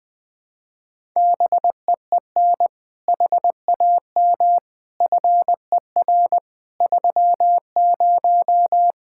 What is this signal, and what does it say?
Morse code, heard as a keyed tone, about 21 wpm, 700 Hz. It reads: BEEN HAM FER 30
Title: Speed-Racing Common QSO Phrases - 20wpm